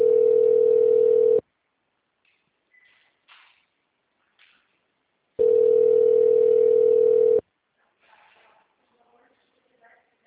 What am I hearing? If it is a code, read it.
Ringback tone (North America), 440 + 480 Hz.